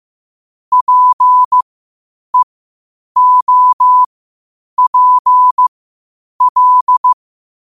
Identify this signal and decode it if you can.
Morse code, heard as a keyed tone, about 15 wpm, spaced as for 8 wpm, 1000 Hz. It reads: PEOPL
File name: qotdfiles / 8wpm